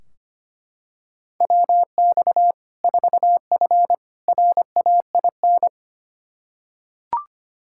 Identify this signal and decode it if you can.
Morse code, heard as a keyed tone, about 25 wpm, 700 Hz. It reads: WX 4F RAIN
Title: Practice QSO - Exchange 2 - Info - 25wpm